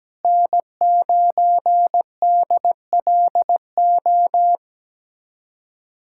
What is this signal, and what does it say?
Morse code, heard as a keyed tone, about 17 wpm, 700 Hz. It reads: N9DLO
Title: USA Callsign DE Callsign - 5 Characters - 17wpm